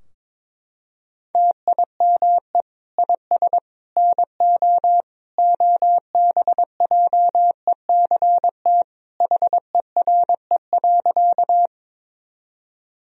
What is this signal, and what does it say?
Morse code, heard as a keyed tone, about 22 wpm, 700 Hz. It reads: TIME IS NO OBJECT HERE.
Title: Sentences from Top 1000 Words - Review Part 3 - 22wpm